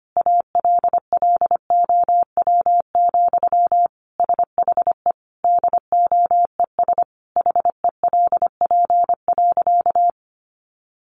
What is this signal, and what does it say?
Morse code, heard as a keyed tone, about 25 wpm, 700 Hz. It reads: ALLOW, SHE DOES HELP.
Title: Sentences from Top 700 Words 25wpm